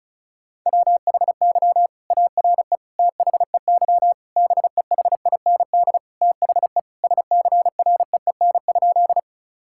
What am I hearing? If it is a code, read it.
Morse code, heard as a keyed tone, about 36 wpm, 700 Hz. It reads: WHY ARE THEY BEHIND THE SCREEN?
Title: Sentences from Top 1000 Words - Review Part 6 - 35wpm